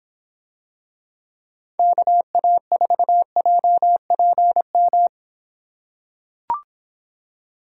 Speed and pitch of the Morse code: 24 wpm, 700 Hz

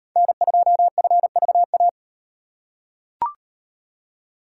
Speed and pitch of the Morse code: 36 wpm, 700 Hz

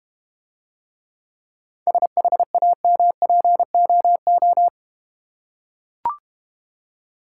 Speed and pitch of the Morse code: 32 wpm, 700 Hz

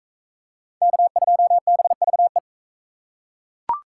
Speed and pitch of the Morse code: 42 wpm, 700 Hz

